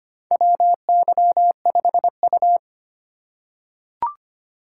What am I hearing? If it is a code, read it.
Morse code, heard as a keyed tone, about 25 wpm, 700 Hz. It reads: WY5U
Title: USA Call Signs - 4x Character Spacing - 25wpm